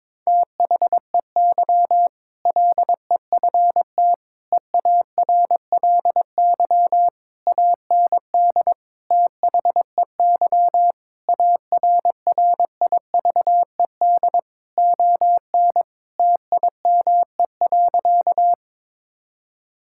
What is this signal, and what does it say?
Morse code, heard as a keyed tone, about 22 wpm, 700 Hz. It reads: THEY LEFT EARLY AND THEY ARRIVED ON TIME.
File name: practice / 22wpm / Sentences with 2-Word Phrases - 22wpm